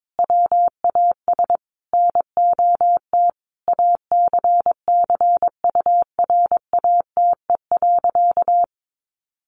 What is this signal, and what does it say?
Morse code, heard as a keyed tone, about 22 wpm, 700 Hz. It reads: WAS NOT ACCURATE.
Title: Sentences from Top 1200 Words 22wpm